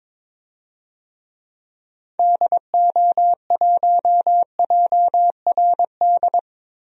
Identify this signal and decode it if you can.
Morse code, heard as a keyed tone, about 22 wpm, 700 Hz. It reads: DO1JRD
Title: DXCC-100 Call Signs - 6 Characters - Repeated 3x - 22wpm